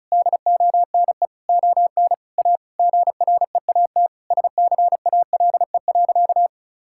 Morse code, heard as a keyed tone, about 36 wpm, 700 Hz. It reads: DONE ON A GREAT SCALE.
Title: Sentences from Top 1000 Words - Review Part 5 - 35wpm